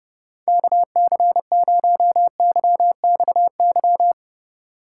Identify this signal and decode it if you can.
Morse code, heard as a keyed tone, about 29 wpm, 700 Hz. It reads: KC0YXY